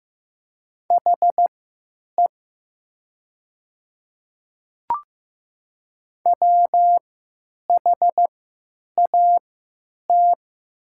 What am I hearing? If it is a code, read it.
Morse code, heard as a keyed tone, about 15 wpm, spaced as for 9 wpm, 700 Hz. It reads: HE WHAT
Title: 3x Character Spacing - Top 100 Words - 15wpm